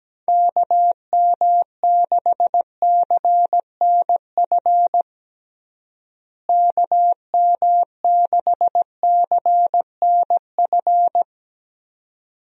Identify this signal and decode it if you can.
Morse code, heard as a keyed tone, about 17 wpm, 700 Hz. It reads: KM6CNF KM6CNF
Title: USA Call Signs - 6 Characters - Repeated 2x - 17wpm